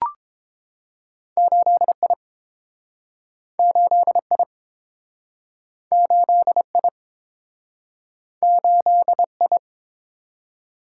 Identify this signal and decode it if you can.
Morse code, heard as a keyed tone, about 30 wpm, 700 Hz. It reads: 8I 8I 8I 8I